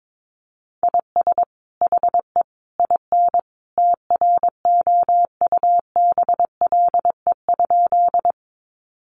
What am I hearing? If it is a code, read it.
Morse code, heard as a keyed tone, about 22 wpm, 700 Hz. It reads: IS HE IN TROUBLE?
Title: Sentences from Top 1200 Words 22wpm